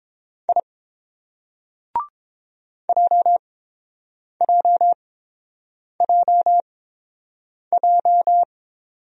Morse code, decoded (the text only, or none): I JJJJ